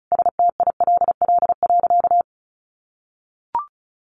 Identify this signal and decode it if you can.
Morse code, heard as a keyed tone, about 34 wpm, 700 Hz. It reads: STILL.
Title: Sentences from Top 1500 Words 35wpm